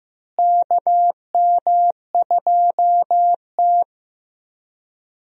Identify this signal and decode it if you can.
Morse code, heard as a keyed tone, about 16 wpm, 700 Hz. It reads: KM2T
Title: USA Call Signs - 4 Characters - Repeated 2x - 15wpm